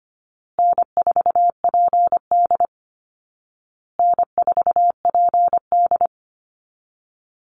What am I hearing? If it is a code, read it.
Morse code, heard as a keyed tone, about 25 wpm, 700 Hz. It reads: N4PD N4PD
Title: Call Signs Repeated 2x - 25wpm